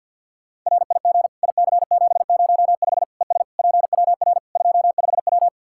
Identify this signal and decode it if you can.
Morse code, heard as a keyed tone, about 50 wpm, 700 Hz. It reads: RIG IC705 ES PWR 25W
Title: Practice QSO - Exchange 2 - Info - 50wpm